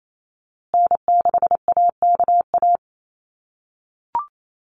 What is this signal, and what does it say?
Morse code, heard as a keyed tone, about 28 wpm, 700 Hz. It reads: N6AKA
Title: USA Call Signs Repeated 2x 28wpm